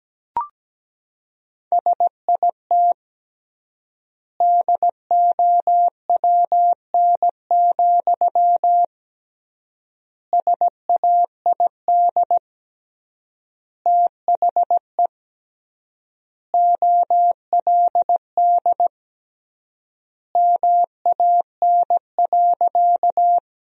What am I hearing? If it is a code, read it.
Morse code, heard as a keyed tone, about 17 wpm, 700 Hz. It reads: SIT DOWN, SAID THE OLD MAN.
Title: 3x Word-Spacing - Sentences from Top 500 Words - 17wpm